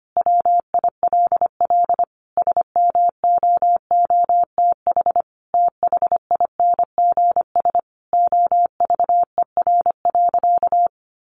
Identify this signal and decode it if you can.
Morse code, heard as a keyed tone, about 25 wpm, 700 Hz. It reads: WILL SMOOTH THINGS OVER.
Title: Sentences from Top 1400 Words 25wpm